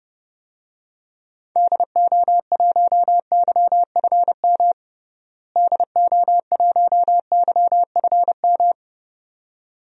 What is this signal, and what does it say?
Morse code, heard as a keyed tone, about 32 wpm, 700 Hz. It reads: DO1YFM DO1YFM
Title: DXCC-100 Call Signs - 6 Characters - Repeated 2x - 30wpm